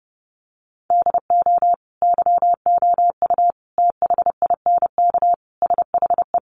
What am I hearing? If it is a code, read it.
Morse code, heard as a keyed tone, about 28 wpm, 700 Hz. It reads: DO YOU THINK SHE
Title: Sentences from Top 200 Words 30wpm